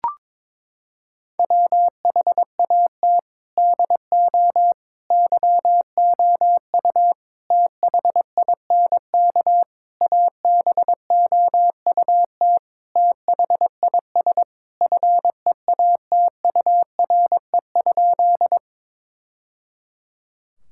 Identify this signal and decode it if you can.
Morse code, heard as a keyed tone, about 22 wpm, 700 Hz. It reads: WHAT DO YOU THINK ABOUT THIS FEATURE?
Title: Sentences from Top 1000 Words - Review Part 5 - 22wpm